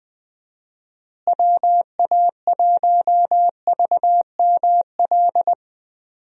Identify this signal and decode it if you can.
Morse code, heard as a keyed tone, about 20 wpm, 700 Hz. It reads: WA1VML
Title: USA Call Signs - 6 Characters - Repeated 3x - 20wpm